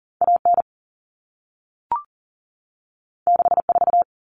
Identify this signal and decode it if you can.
Morse code, heard as a keyed tone, about 40 wpm, 700 Hz. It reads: AN 64
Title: Common Cut Numbers - 2-Digit 40wpm